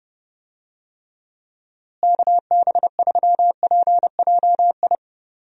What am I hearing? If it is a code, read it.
Morse code, heard as a keyed tone, about 31 wpm, 700 Hz. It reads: KB3PJI
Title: USA Call Signs - 6 Characters - Repeated 2x - 30wpm